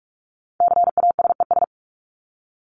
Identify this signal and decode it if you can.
Morse code, heard as a keyed tone, about 44 wpm, 700 Hz. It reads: CASES
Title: ES:S Suffix - Top 100 Words with Base Words 45wpm